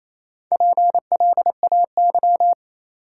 Morse code, heard as a keyed tone, about 27 wpm, 690 Hz. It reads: PLAY
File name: Easy Words - Speed-Racing with Hints - 22wpm